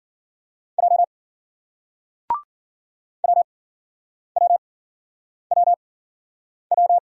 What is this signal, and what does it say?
Morse code, heard as a keyed tone, about 56 wpm, spaced as for 9 wpm, 700 Hz. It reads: J WWWW